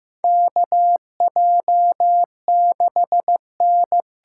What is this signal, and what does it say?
Morse code, heard as a keyed tone, about 15 wpm, 700 Hz. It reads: KJ6N
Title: USA Call Signs - 6 Characters - 6x Character Spacing - 15wpm